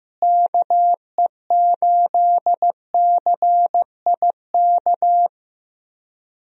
Morse code, heard as a keyed tone, about 15 wpm, 700 Hz. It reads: KE8CIK
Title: USA Call Signs - 6 Characters - Repeated 3x - 15wpm